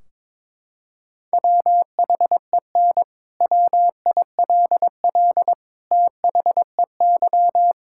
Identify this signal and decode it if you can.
Morse code, heard as a keyed tone, about 22 wpm, 700 Hz. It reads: WHEN WILL THEY
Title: Sentences from Top 1600 Words 22wpm